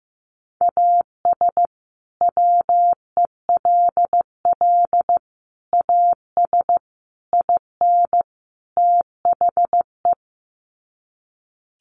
Morse code, heard as a keyed tone, about 15 wpm, 700 Hz. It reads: AS WELL AS IN THE
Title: Sets of 5 Words Encore 15wpm